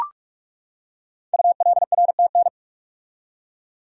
Morse code, heard as a keyed tone, about 44 wpm, 700 Hz. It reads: ULRTN
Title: Code Groups - 5 Letters - 45wpm